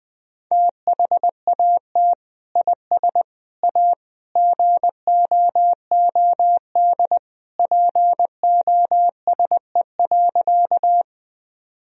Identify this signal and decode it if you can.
Morse code, heard as a keyed tone, about 21 wpm, 700 Hz. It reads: THAT IS A GOOD POSE.
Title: Sentences from Top 1000 Words - Review Part 2 - 20wpm